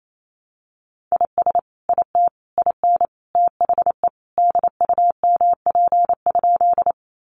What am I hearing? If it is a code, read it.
Morse code, heard as a keyed tone, about 28 wpm, 700 Hz. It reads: IS IT IN THE DUMP?
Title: Sentences from Top 1300 Words 28wpm